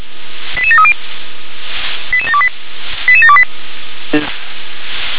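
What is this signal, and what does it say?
Five-tone selective call (ZVEI), heard as quick runs of five tones, tones of about 70 ms. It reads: 88730 88728 88728
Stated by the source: ZVEI five-tone call received by SDR.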